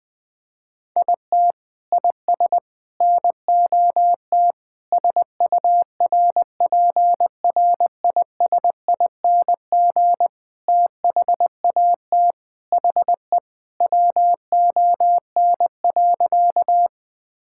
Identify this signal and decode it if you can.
Morse code, heard as a keyed tone, about 20 wpm, 700 Hz. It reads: IT IS NOT SURPRISING THAT HE WON.
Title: Sentences with 5-Word Phrases - 20wpm